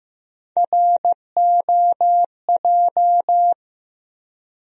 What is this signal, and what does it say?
Morse code, heard as a keyed tone, about 15 wpm, 700 Hz. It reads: ROJ
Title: USA Call Sign Postfixes - 3 Letters 15wpm